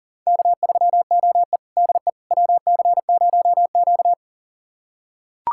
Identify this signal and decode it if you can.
Morse code, heard as a keyed tone, about 40 wpm, 700 Hz. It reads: K3OE DE WC0Q
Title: USA Callsign DE Callsign - 4 Characters - 40wpm